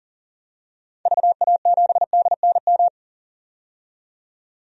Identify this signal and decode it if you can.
Morse code, heard as a keyed tone, about 40 wpm, 700 Hz. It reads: VA7DNM
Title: DXCC-100 Call Signs - 6 Characters - 40wpm